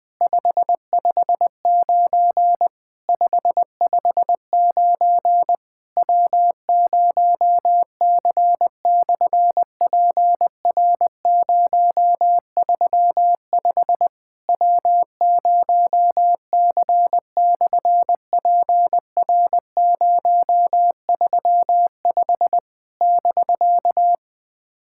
Morse code, heard as a keyed tone, about 20 wpm, 700 Hz. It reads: 559 559 W0C/PR035 W0C/PR035 <BK>
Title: SOTA QSO Practice 20wpm